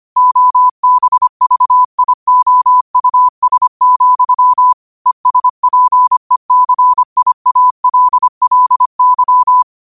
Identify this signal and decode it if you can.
Morse code, heard as a keyed tone, about 25 wpm, 1000 Hz. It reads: OBVIOUS, ESPECIALLY
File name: qotdfiles / 25wpm